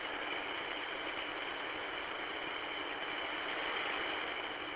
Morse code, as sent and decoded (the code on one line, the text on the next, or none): none